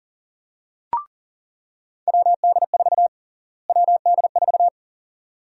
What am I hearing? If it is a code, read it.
Morse code, heard as a keyed tone, about 40 wpm, 700 Hz. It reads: WD4 WD4
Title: USA Call Sign Prefixes - Repeated 2x - 40wpm